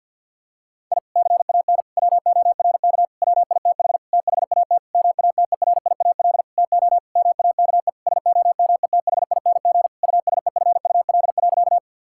Morse code, heard as a keyed tone, about 52 wpm, 700 Hz. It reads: I CAN WORK WITH THAT MATERIAL TO MAKE SOMETHING USEFUL.
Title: Sentences from Top 1000 Words - Review Part 5 - 50wpm